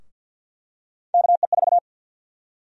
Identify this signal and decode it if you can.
Morse code, heard as a keyed tone, about 53 wpm, 700 Hz. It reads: KE4
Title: USA Call Sign Prefixes - Repeated 2x - 50wpm